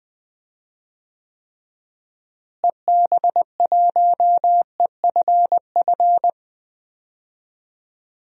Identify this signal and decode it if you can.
Morse code, heard as a keyed tone, about 20 wpm, 700 Hz. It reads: EB1EFF
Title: DXCC-100 Call Signs - 6 Characters - Repeated 3x - 20wpm